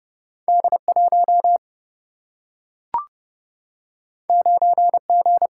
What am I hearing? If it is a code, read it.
Morse code, heard as a keyed tone, about 30 wpm, 700 Hz. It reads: D 1 9 G
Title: Single-Letter Single-Number 30wpm